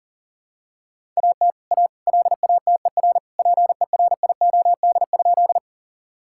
Morse code, heard as a keyed tone, about 37 wpm, 700 Hz. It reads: AT A LATER PERIOD?
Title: Sentences from Top 1000 Words - Review Part 5 - 40wpm